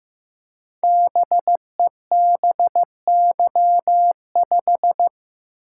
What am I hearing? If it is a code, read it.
Morse code, heard as a keyed tone, about 15 wpm, 700 Hz. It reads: BEBY5
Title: Code Groups - 5 Letter:Numbers - 15wpm